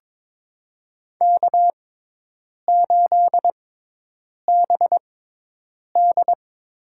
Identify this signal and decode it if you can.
Morse code, heard as a keyed tone, about 22 wpm, spaced as for 8 wpm, 700 Hz. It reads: K8BD
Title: USA Call Signs - 5 Characters - 6x Character Spacing - 22wpm